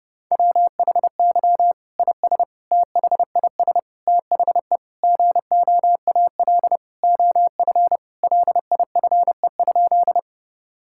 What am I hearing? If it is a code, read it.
Morse code, heard as a keyed tone, about 30 wpm, 700 Hz. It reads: WHY IS THIS THE GOAL OF LIFE?
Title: Sentences from Top 1000 Words - Review Part 5 - 30wpm